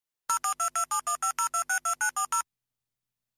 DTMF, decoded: 04227150262940